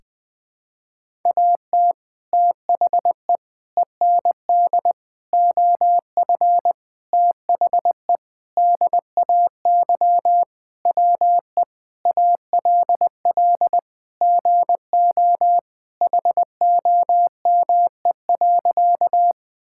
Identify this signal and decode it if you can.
Morse code, heard as a keyed tone, about 20 wpm, 700 Hz. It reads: AT THE END OF THE DAY WE ALL GO HOME.
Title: Sentences with 4-Word Phrases - 20wpm